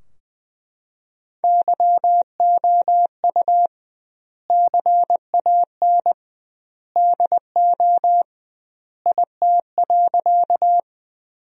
Morse code, heard as a keyed tone, about 20 wpm, 700 Hz. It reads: YOU CAN DO IT.
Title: 2x Word-Spacing - Sentences from Top 100 Words - 20wpm